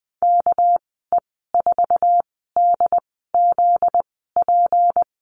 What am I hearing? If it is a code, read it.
Morse code, heard as a keyed tone, about 20 wpm, spaced as for 14 wpm, 700 Hz. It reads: KE4DZP
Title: 245 Lesson 39g - Letter z - Call Signs 20wpm